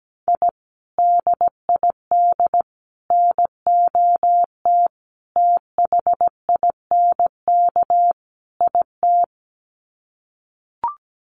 Morse code, heard as a keyed tone, about 17 wpm, 700 Hz. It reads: I DID NOT THINK IT
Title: Sets of 5 Words Encore2 17wpm